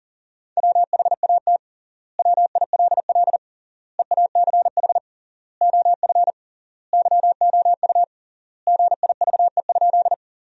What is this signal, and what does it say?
Morse code, heard as a keyed tone, about 41 wpm, 700 Hz. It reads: WHAT WILL EACH OF YOU GIVE?
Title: 3x Word-Spacing - Sentences from Top 200 Words - 40wpm